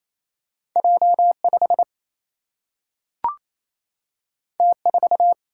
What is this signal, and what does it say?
Morse code, heard as a keyed tone, about 28 wpm, 700 Hz. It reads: J 5 T 4
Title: Single-Letter Single-Number 28wpm